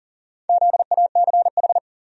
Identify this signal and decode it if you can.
Morse code, heard as a keyed tone, about 37 wpm, 700 Hz. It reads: ZACH
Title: USA Common Names - Max 4 Letters - 40wpm